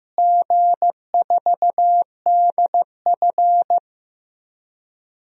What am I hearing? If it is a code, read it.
Morse code, heard as a keyed tone, about 15 wpm, 700 Hz. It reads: G4DF
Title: DXCC-100 Call Signs - 4 Characters - Repeated 2x - 15wpm